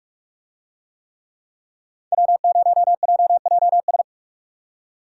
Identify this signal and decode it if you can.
Morse code, heard as a keyed tone, about 44 wpm, 700 Hz. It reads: W0JJS